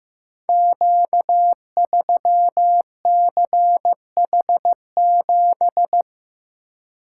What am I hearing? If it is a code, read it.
Morse code, heard as a keyed tone, about 15 wpm, 700 Hz. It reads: Q3CH7
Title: Code Groups - 5 Letter:Numbers - 15wpm